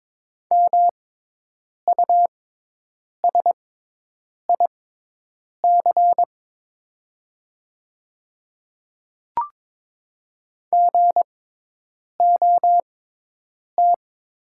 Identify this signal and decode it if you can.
Morse code, heard as a keyed tone, about 22 wpm, spaced as for 8 wpm, 700 Hz. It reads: MUSIC GOT
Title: 6x Character Spacing - Top 300 Words - 22wpm